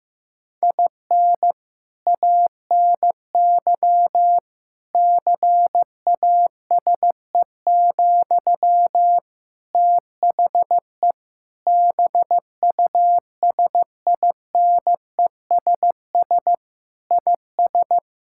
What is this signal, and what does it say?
Morse code, heard as a keyed tone, about 15 wpm, 700 Hz. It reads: IN ANY CASE, THE BUSINESS IS
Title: Sentences from Top 1000 Words - Review Part 5 - 15wpm